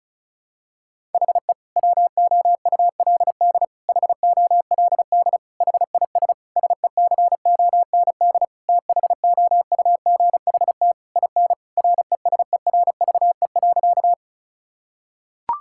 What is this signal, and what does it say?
Morse code, heard as a keyed tone, about 34 wpm, 700 Hz. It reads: HE WOULD HOLD HIS SECOND THOUGHT IN RESERVE.
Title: Sentences from Top 1000 Words - Review Part 6 - 35wpm